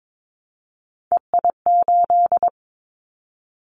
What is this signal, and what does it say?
Morse code, heard as a keyed tone, about 22 wpm, 700 Hz. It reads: EI8
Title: Callsign Prefixes for Top 50 DXCC - 22wpm